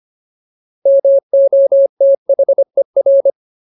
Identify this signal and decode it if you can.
Morse code, heard as a keyed tone, about 25 wpm, 550 Hz. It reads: MOTHER
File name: Ultimate Word List / 25WPM Ultimate